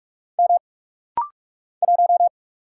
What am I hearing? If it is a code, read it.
Morse code, heard as a keyed tone, about 46 wpm, 700 Hz. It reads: M 1
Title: Single Letter-Number - Rapid-Fire 45wpm